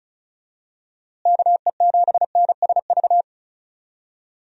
Morse code, heard as a keyed tone, about 34 wpm, 700 Hz. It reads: KE7NSV
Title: USA Call Signs - 6 Characters - Repeated 2x - 35wpm